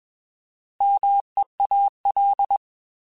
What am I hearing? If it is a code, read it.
Morse code, heard as a keyed tone, about 21 wpm, 800 Hz. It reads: MEAL
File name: Alphabetical / Alphabetical mp3s / Ms @20